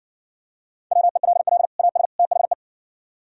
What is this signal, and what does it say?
Morse code, heard as a keyed tone, about 55 wpm, 700 Hz. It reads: WELL AS THE